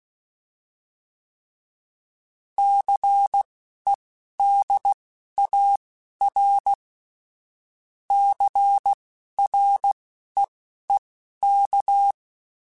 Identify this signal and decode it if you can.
Morse code, heard as a keyed tone, about 16 wpm, spaced as for 11 wpm, 800 Hz. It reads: CEDAR CREEK